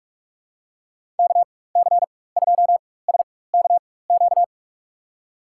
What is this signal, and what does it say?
Morse code, heard as a keyed tone, about 47 wpm, spaced as for 21 wpm, 700 Hz. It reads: KC2SKQ
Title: USA Call Signs - 4x Character Spacing - 45wpm